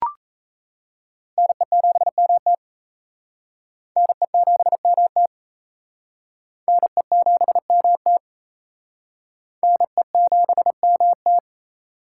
Morse code, heard as a keyed tone, about 37 wpm, 700 Hz. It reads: NE7MT NE7MT NE7MT NE7MT